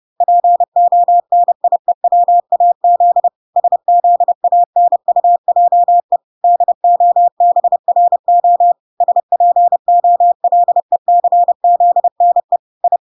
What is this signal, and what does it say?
Morse code, heard as a keyed tone, about 30 wpm, 700 Hz. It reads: PONIEWAZ SZANUJE DOBRO SPOLECZNE I